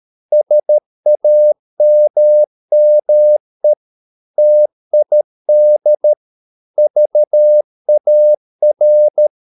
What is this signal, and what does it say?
Morse code, heard as a keyed tone, about 13 wpm, 600 Hz. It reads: SAMME TID VAR
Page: Ferie 13 wpm